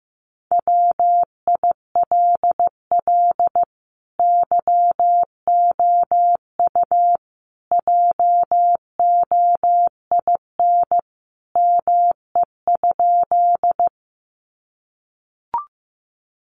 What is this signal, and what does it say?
Morse code, heard as a keyed tone, about 15 wpm, 700 Hz. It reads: WILL YOU JOIN ME?
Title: Sentences from Top 1400 Words 15wpm